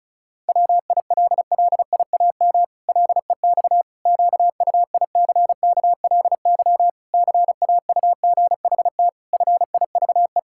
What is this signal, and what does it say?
Morse code, heard as a keyed tone, about 36 wpm, 700 Hz. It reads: WILLIAM LEX QUICKLY CAUGHT FIVE
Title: Pangram Challenge v2 - 35wpm